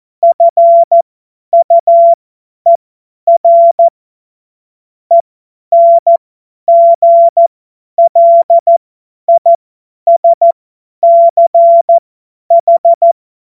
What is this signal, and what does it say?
Morse code, heard as a keyed tone, about 14 wpm, spaced as for 10 wpm, 680 Hz. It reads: FUER ENGLISCH
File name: Morsekurs / Witze / Witz 21. Juni 2022 50 F70